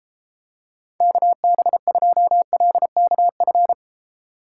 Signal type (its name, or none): Morse code